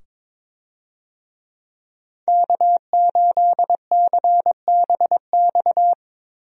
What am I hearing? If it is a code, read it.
Morse code, heard as a keyed tone, about 22 wpm, 700 Hz. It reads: K8CBX
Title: USA Call Signs - 5 Characters - 22wpm